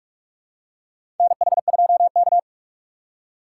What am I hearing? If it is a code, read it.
Morse code, heard as a keyed tone, about 42 wpm, 700 Hz. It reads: NH2K
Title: USA Call Signs - 2x Character Spacing - 45wpm